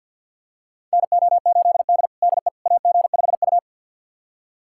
Morse code, heard as a keyed tone, about 45 wpm, 700 Hz. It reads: NO8D DE AG5U
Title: USA Callsign DE Callsign - 4 Characters - 50wpm